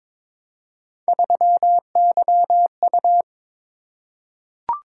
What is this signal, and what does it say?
Morse code, heard as a keyed tone, about 22 wpm, 700 Hz. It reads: 3YU
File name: practice / 22wpm / Three Letter-Number Combinations - 22wpm